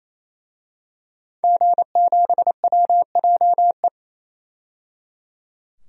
Morse code, heard as a keyed tone, about 28 wpm, 700 Hz. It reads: G7WJE
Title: DXCC-100 Call Signs - 5 Characters - 28wpm